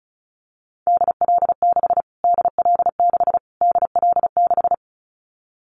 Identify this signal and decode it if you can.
Morse code, heard as a keyed tone, about 34 wpm, 700 Hz. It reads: DL6 DL6 DL6
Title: Callsign Prefixes for Top 25 DXCC - Repeated 3x - 35wpm